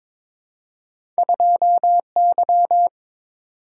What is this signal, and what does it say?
Morse code, heard as a keyed tone, about 22 wpm, 700 Hz. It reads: 2Y
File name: practice / 22wpm / Two Letter-Number Combinations - 22wpm